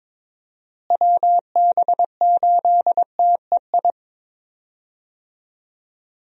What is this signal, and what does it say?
Morse code, heard as a keyed tone, about 22 wpm, 700 Hz. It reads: WB8TEI
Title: USA Call Signs - 6 Characters - Repeated 2x - 22wpm